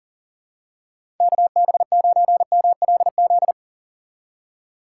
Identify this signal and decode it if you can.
Morse code, heard as a keyed tone, about 39 wpm, 700 Hz. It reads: KB9MLZ
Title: USA Call Signs - 6 Characters - 2x Character Spacing - 40wpm